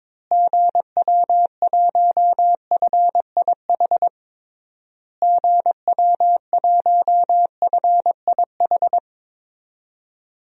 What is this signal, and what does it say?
Morse code, heard as a keyed tone, about 22 wpm, 700 Hz. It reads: GW1FIH GW1FIH
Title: DXCC-100 Call Signs - 6 Characters - Repeated 3x - 22wpm